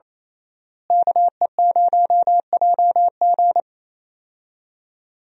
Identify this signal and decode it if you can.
Morse code, heard as a keyed tone, about 27 wpm, 700 Hz. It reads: KE0JG
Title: USA Call Signs - 5 Characters - Repeated 3x - 28wpm